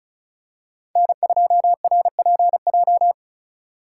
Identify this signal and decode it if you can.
Morse code, heard as a keyed tone, about 34 wpm, 700 Hz. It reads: N2RPJ
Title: USA Call Signs - 5 Characters - 2x Character Spacing - 35wpm